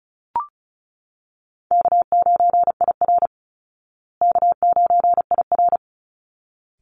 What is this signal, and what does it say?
Morse code, heard as a keyed tone, about 34 wpm, 700 Hz. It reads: K9IR K9IR
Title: Call Signs Repeated 3x - 35wpm